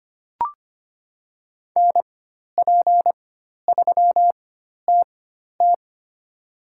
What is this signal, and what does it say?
Morse code, heard as a keyed tone, about 25 wpm, spaced as for 12 wpm, 700 Hz. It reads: NP3TT
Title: USA Call Signs - 4x Character Spacing - 25wpm